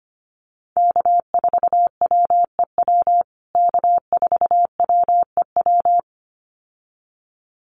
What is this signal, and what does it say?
Morse code, heard as a keyed tone, about 25 wpm, 700 Hz. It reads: K4WEW K4WEW
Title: USA Call Signs Repeated 2x 25wpm